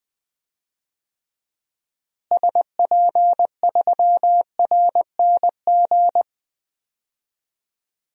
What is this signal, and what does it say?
Morse code, heard as a keyed tone, about 20 wpm, 700 Hz. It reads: SP3RNG
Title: DXCC-100 Call Signs - 6 Characters - Repeated 3x - 20wpm